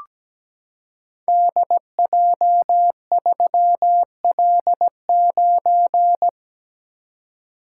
Morse code, heard as a keyed tone, about 17 wpm, 700 Hz. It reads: DJ3L9
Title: Code Groups - 5 Letter:Numbers - 17wpm